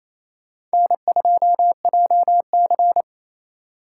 Morse code, heard as a keyed tone, about 35 wpm, 700 Hz. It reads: N2JC
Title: Speed Racing - USA 1x2 Call Signs 28wpm